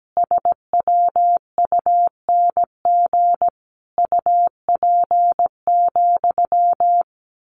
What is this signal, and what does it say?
Morse code, heard as a keyed tone, about 17 wpm, 700 Hz. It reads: SWUNG UP,
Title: A Sound Of Thunder 17wpm